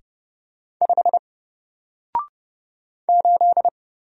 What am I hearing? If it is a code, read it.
Morse code, heard as a keyed tone, about 30 wpm, 700 Hz. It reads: 5 8
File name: Single-Letter-Number-30wpm